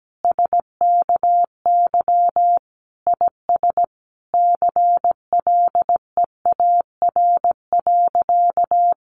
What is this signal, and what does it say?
Morse code, heard as a keyed tone, about 17 wpm, 700 Hz. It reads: SKY IS CLEAR.
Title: Sentences from Top 1200 Words 17wpm